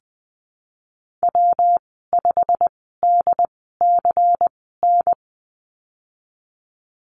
Morse code, heard as a keyed tone, about 20 wpm, spaced as for 15 wpm, 700 Hz. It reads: W5DCN
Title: 186 Lesson 30g - Letter b - Call Signs 20wpm